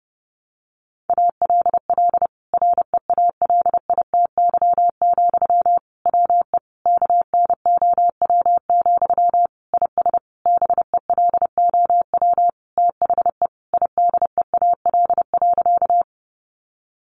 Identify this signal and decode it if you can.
Morse code, heard as a keyed tone, about 30 wpm, 700 Hz. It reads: ALL REALITY, WE KNOW, IS BELOW THE IDEAL.
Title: Sentences from Top 1400 Words 30wpm